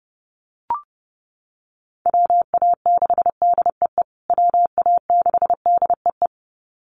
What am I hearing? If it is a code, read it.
Morse code, heard as a keyed tone, about 28 wpm, 700 Hz. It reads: WA6DEE WA6DEE
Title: USA Call Signs Repeated 2x 30wpm